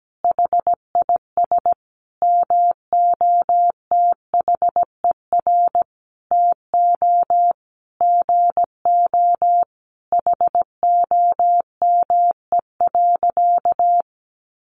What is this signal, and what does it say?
Morse code, heard as a keyed tone, about 17 wpm, 700 Hz. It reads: HIS MOTHER TO GO HOME.
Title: Sentences from Top 1400 Words 17wpm